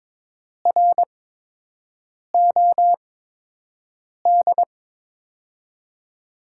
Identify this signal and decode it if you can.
Morse code, heard as a keyed tone, about 23 wpm, spaced as for 6 wpm, 700 Hz. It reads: ROD
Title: USA Call Signs - 6 Characters - 8x Character Spacing - 22wpm